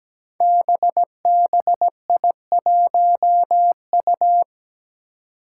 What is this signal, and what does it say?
Morse code, heard as a keyed tone, about 17 wpm, 700 Hz. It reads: BBI1U
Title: Five Challenging Letter-Number Combinations 17wpm